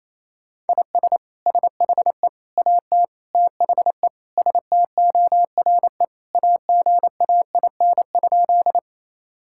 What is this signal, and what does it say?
Morse code, heard as a keyed tone, about 28 wpm, 700 Hz. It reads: IS SHE AT THE STORE AGAIN?
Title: Sentences from Top 1000 Words - Review Part 5 - 28wpm